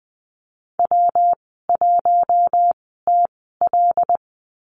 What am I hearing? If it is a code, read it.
Morse code, heard as a keyed tone, about 20 wpm, spaced as for 15 wpm, 700 Hz. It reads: W1TL
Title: 122 Lesson 20g - Number 3 - Call Signs 20wpm